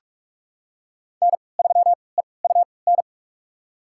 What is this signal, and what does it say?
Morse code, heard as a keyed tone, about 44 wpm, spaced as for 25 wpm, 700 Hz. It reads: N3EUN
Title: USA Call Signs - 5 Characters - 3x Character Spacing - 45wpm